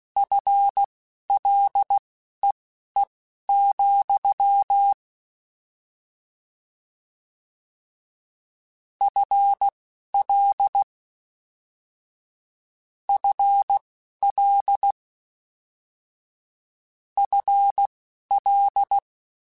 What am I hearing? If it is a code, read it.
Morse code, heard as a keyed tone, about 16 wpm, spaced as for 12 wpm, 800 Hz. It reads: FLEE, FL FL FL